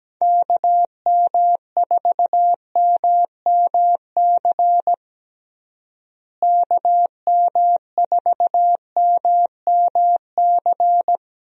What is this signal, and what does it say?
Morse code, heard as a keyed tone, about 17 wpm, 700 Hz. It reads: KM4MMC KM4MMC
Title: USA Call Signs - 6 Characters - Repeated 2x - 17wpm